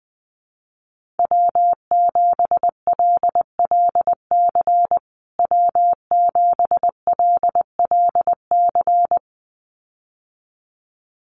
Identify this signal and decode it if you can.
Morse code, heard as a keyed tone, about 20 wpm, 700 Hz. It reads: W7LLC W7LLC
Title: USA Call Signs Repeated 2x 20wpm